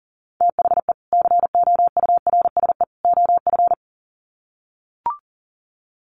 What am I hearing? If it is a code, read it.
Morse code, heard as a keyed tone, about 40 wpm, 700 Hz. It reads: THE COURSE OF
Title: Sets of 3 Words 40wpm